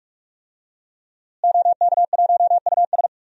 Morse code, heard as a keyed tone, about 41 wpm, 700 Hz. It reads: OK1US
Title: DXCC-100 Call Signs - 5 Characters - Repeated 3x - 45wpm